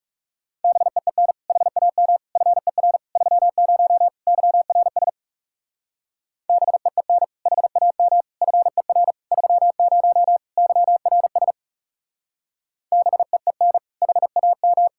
Morse code, heard as a keyed tone, about 37 wpm, 700 Hz. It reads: BEEN HAM FER 30 YRS BEEN HAM FER 30 YRS BEEN HAM
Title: Speed-Racing Common QSO Phrases - 30wpm